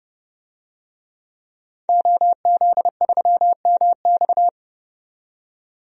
Morse code, heard as a keyed tone, about 30 wpm, 700 Hz. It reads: OZ3MX
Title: DXCC-100 Call Signs - 5 Characters - 30wpm